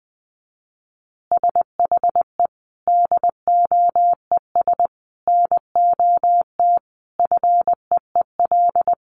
Morse code, heard as a keyed tone, about 20 wpm, 700 Hz. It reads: SHE DOES NOT FEEL
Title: Additional Easy Phrases 20wpm